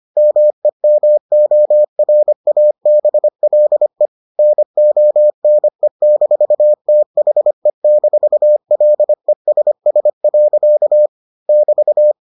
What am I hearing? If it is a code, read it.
Morse code, heard as a keyed tone, about 25 wpm, 600 Hz. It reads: MEMORABLE NONE-THE-LESS. <BT>